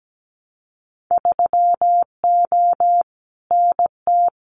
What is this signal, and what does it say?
Morse code, heard as a keyed tone, about 17 wpm, 700 Hz. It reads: 3O NT
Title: Winter Field Day Exchanges Encore 17wpm